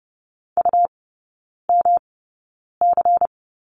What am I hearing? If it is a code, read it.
Morse code, heard as a keyed tone, about 30 wpm, spaced as for 9 wpm, 700 Hz. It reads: UMC
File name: Lesson 18e - Letter m - Three Characters 30wpm